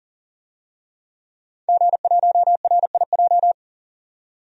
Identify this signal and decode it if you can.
Morse code, heard as a keyed tone, about 40 wpm, 700 Hz. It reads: G1RIJ
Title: DXCC-100 Call Signs - 5 Characters - 40wpm